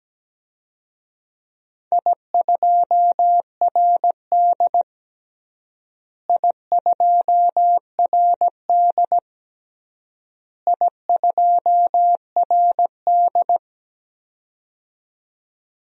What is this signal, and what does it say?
Morse code, heard as a keyed tone, about 17 wpm, 700 Hz. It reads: I2RD I2RD I2RD
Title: DXCC-100 Call Signs - 4 Characters - Repeated 3x - 17wpm